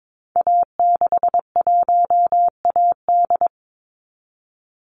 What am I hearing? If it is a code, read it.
Morse code, heard as a keyed tone, about 22 wpm, 700 Hz. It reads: A61AD
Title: Call Signs Repeated 3x - 22wpm